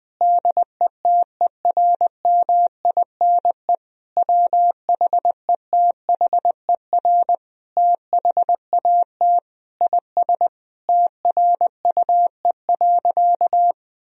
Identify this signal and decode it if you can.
Morse code, heard as a keyed tone, about 20 wpm, 700 Hz. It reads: DETERMINE WHETHER THAT IS TRUE.
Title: Sentences from Top 1000 Words - Review Part 4 - 20wpm